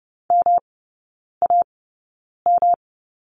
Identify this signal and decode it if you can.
Morse code, heard as a keyed tone, about 30 wpm, spaced as for 9 wpm, 700 Hz. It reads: MAM
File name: Lesson 18e - Letter m - Three Characters 30wpm